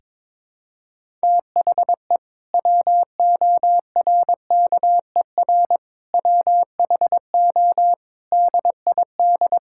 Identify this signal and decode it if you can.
Morse code, heard as a keyed tone, about 22 wpm, 700 Hz. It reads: THE WORKER WHO DID
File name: Sentences with 7-Word Phrases - 22wpm